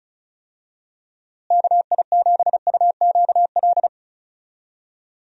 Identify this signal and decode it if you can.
Morse code, heard as a keyed tone, about 36 wpm, 700 Hz. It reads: KI7UQL
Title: USA Call Signs - 6 Characters - Repeated 2x - 35wpm